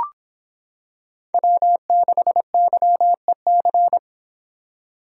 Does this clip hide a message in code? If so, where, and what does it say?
Yes: Morse code from 1.3 to 4.0 s, heard as a keyed tone, about 23 wpm, 700 Hz. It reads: W6YEC